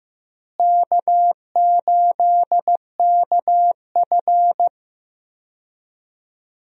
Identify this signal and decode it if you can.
Morse code, heard as a keyed tone, about 16 wpm, 700 Hz. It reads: K8KF
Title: USA Call Signs - 4 Characters - Repeated 2x - 15wpm